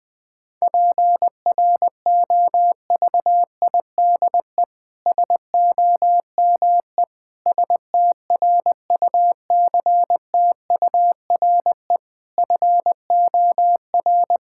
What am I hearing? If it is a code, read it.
Morse code, heard as a keyed tone, about 21 wpm, 700 Hz. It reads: PROVIDE SOME STRUCTURE FOR
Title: Sentences from Top 1000 Words - Review Part 6 - 20wpm